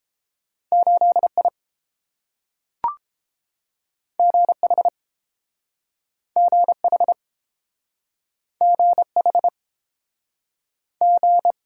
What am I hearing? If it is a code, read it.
Morse code, heard as a keyed tone, about 30 wpm, 700 Hz. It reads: 8 I G H G H G H G